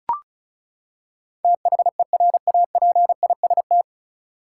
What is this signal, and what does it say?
Morse code, heard as a keyed tone, about 34 wpm, 700 Hz. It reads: THERAPIST
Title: Practice QSO - Exchange 3 - More Info - 35wpm